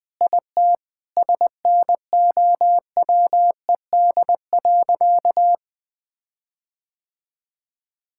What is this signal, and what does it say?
Morse code, heard as a keyed tone, about 21 wpm, 700 Hz. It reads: IT SNOWED.
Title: Sentences with 5-Word Phrases - Spoken Beginning - 20wpm